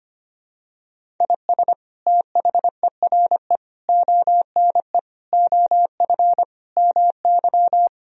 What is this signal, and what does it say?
Morse code, heard as a keyed tone, about 25 wpm, 700 Hz. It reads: IS THERE ONE OF MY
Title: Sentences from Top 1000 Words - Review Part 2 - 25wpm